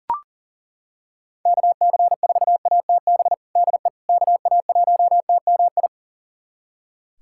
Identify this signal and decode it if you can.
Morse code, heard as a keyed tone, about 40 wpm, 700 Hz. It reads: KC4ATB DE KA1TMI
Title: USA Callsign DE Callsign - 6 Characters - 40wpm